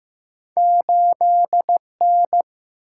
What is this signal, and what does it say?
Morse code, heard as a keyed tone, about 15 wpm, 700 Hz. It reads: 8N
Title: Single-Letter Single-Number 15wpm